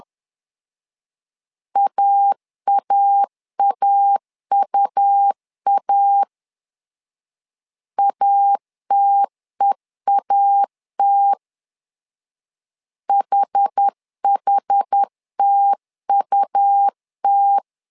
Morse code, heard as a keyed tone, about 10 wpm, 800 Hz. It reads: AAAUA ATEAT HHTUT
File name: LEZIONI / 2 UHDN / 2 UH 10wpm 1714